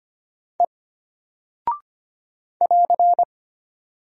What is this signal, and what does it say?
Morse code, heard as a keyed tone, about 25 wpm, 700 Hz. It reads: E <AR>
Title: Single Character (Letter:Number:Punctuation:Prosign) Alt 25wpm